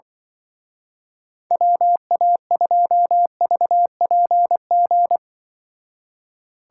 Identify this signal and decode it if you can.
Morse code, heard as a keyed tone, about 24 wpm, 700 Hz. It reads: WA2VPG
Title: Speed Racing - USA 2x3 Call Signs 20wpm